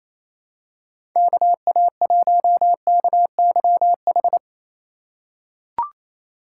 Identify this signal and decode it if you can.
Morse code, heard as a keyed tone, about 28 wpm, 700 Hz. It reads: KA1KYH
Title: USA Call Signs - 6 Characters - 4x Character Spacing - 28wpm